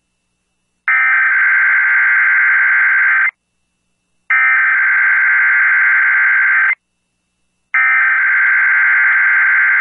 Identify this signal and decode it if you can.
Emergency Alert System (SAME) tones (header bursts), heard as harsh buzzing bursts. It is heard from 0.9 to 9.5 s.